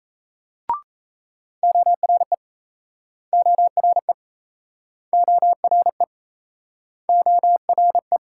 Morse code, heard as a keyed tone, about 39 wpm, 700 Hz. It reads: ORE ORE ORE ORE